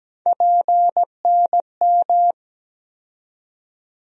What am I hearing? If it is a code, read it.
Morse code, heard as a keyed tone, about 17 wpm, 700 Hz. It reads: PNM
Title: USA Call Sign Postfixes - 3 Letters 17wpm